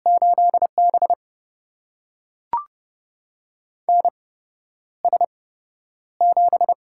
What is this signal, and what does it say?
Morse code, heard as a keyed tone, about 30 wpm, spaced as for 11 wpm, 700 Hz. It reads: 8B NS7
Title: USA Call Signs - 4 Characters - 8x Character Spacing - 30wpm